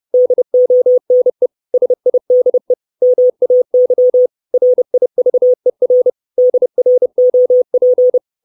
Morse code, heard as a keyed tone, about 30 wpm, 500 Hz. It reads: DONE SIDE MAY RIVER DROP
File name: 632 Words 30WPM